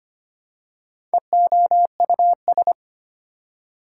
Morse code, heard as a keyed tone, about 25 wpm, 700 Hz. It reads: EOUS
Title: Prefixes and Suffixes 25wpm